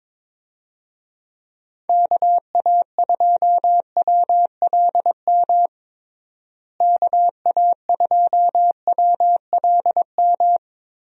Morse code, heard as a keyed tone, about 22 wpm, 700 Hz. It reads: KA2WLM KA2WLM
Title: USA Call Signs - 6 Characters - Repeated 3x - 22wpm